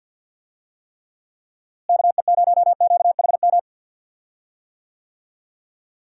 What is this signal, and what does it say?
Morse code, heard as a keyed tone, about 46 wpm, 700 Hz. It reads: KE0QHM
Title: USA Call Signs - 6 Characters - Repeated 2x - 50wpm